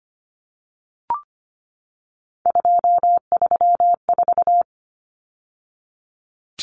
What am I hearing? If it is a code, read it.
Morse code, heard as a keyed tone, about 24 wpm, 700 Hz. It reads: 234
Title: Numbers - 3-Digit 25wpm